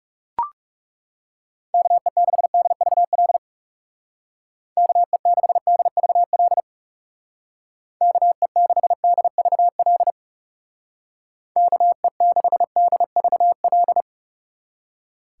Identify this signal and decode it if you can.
Morse code, heard as a keyed tone, about 37 wpm, 700 Hz. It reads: KE6DVL KE6DVL KE6DVL KE6DVL
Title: Speed Racing - USA 2x3 Call Signs 30wpm